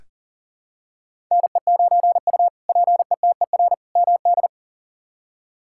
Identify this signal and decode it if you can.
Morse code, heard as a keyed tone, about 40 wpm, 700 Hz. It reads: NE9U PETER MD
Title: K1USN SST QSO Practice - 40wpm